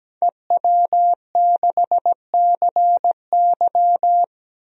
Morse code, heard as a keyed tone, about 17 wpm, 700 Hz. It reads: EW6CY
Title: DXCC-100 Call Signs - 5 Characters - 17wpm